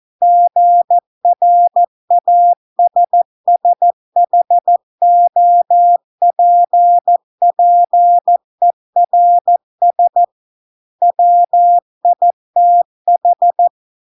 Morse code, heard as a keyed tone, about 14 wpm, 700 Hz. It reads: GRASSHOPPERS WITH